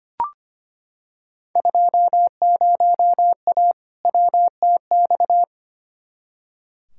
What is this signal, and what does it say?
Morse code, heard as a keyed tone, about 25 wpm, 700 Hz. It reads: 20A WTX
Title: Field Day Exchanges Encore - 25wpm